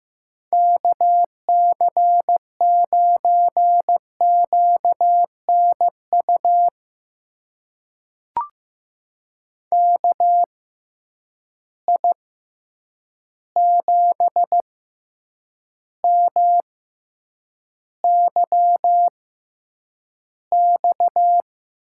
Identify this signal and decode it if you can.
Morse code, heard as a keyed tone, about 15 wpm, spaced as for 8 wpm, 700 Hz. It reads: KC9QNU KI7MYX
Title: USA Call Signs - 6 Characters - 6x Character Spacing - 15wpm